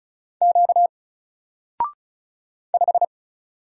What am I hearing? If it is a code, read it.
Morse code, heard as a keyed tone, about 36 wpm, 700 Hz. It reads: Q 5
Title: Single Letter-Number 35wpm